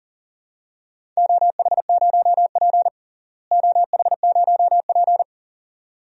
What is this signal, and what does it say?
Morse code, heard as a keyed tone, about 40 wpm, 700 Hz. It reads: OH0P OH0P
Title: DXCC-100 Call Signs - 4 Characters - Repeated 2x - 40wpm